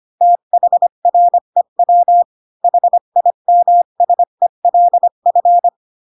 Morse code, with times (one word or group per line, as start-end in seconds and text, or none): 0.2-2.2 THREW
2.6-5.7 HIMSELF